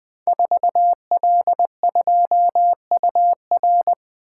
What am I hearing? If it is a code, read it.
Morse code, heard as a keyed tone, about 20 wpm, 700 Hz. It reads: 4L2UR
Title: Code Groups - 5 Letter:Numbers - 20wpm